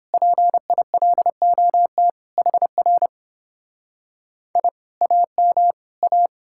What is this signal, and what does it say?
Morse code, heard as a keyed tone, about 31 wpm, 700 Hz. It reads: PILOT HR I AM A